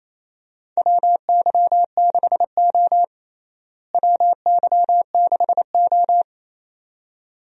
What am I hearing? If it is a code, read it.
Morse code, heard as a keyed tone, about 28 wpm, 700 Hz. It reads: WY6O WY6O
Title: USA Call Signs - 4 Characters - Repeated 2x - 28wpm